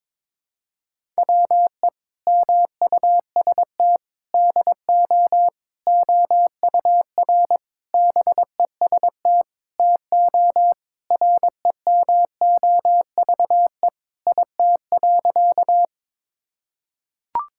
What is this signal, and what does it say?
Morse code, heard as a keyed tone, about 22 wpm, 700 Hz. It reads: WE MUST DO OUR BEST TO REMOVE IT.
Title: Sentences from Top 1000 Words - Review Part 5 - 22wpm